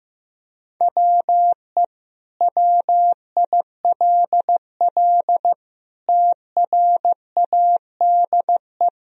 Morse code, heard as a keyed tone, about 15 wpm, 700 Hz. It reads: WE WILL TRADE
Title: Sentences from Top 1000 Words - Review Part 4 - 15wpm